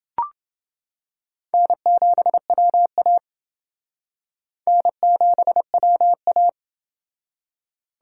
Morse code, heard as a keyed tone, about 28 wpm, 700 Hz. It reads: N7WA N7WA